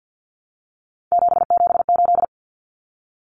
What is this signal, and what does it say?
Morse code, heard as a keyed tone, about 50 wpm, 700 Hz. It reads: 778
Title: Common Cut Numbers - 3-Digit 50wpm